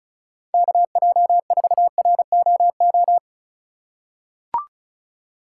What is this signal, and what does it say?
Morse code, heard as a keyed tone, about 35 wpm, 700 Hz. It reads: KJ4ROO
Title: USA Call Signs - 6 Characters - 3x Character Spacing - 35wpm